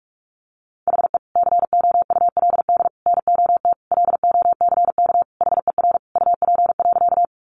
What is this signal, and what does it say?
Morse code, heard as a keyed tone, about 44 wpm, 700 Hz. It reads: HE COULD NOT LOCK HER UP.
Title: Sentences from Top 1300 Words 45wpm